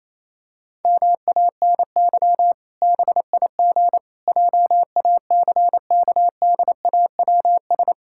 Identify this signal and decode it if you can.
Morse code, heard as a keyed tone, about 28 wpm, 700 Hz. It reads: MANY BIG JACKDAWS
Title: Pangram Challenge v2 - 28wpm